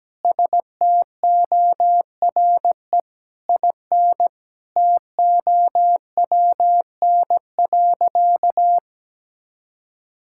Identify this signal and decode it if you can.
Morse code, heard as a keyed tone, about 17 wpm, 700 Hz. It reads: STORE IN TOWN.
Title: Sentences from Top 1000 Words - Review Part 5 - 17wpm